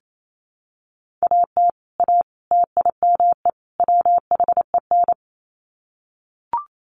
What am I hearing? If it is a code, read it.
Morse code, heard as a keyed tone, about 28 wpm, 700 Hz. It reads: AT A TIME WHEN
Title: Sets of 4 Words 28wpm